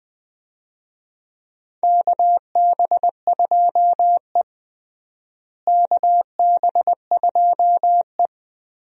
Morse code, heard as a keyed tone, about 20 wpm, 700 Hz. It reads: KB2E KB2E
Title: USA Call Signs - 4 Characters - Repeated 3x - 20wpm